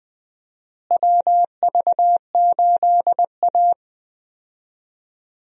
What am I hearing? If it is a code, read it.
Morse code, heard as a keyed tone, about 21 wpm, 700 Hz. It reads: WV8A
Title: USA Call Signs - 4 Characters - Repeated 3x - 20wpm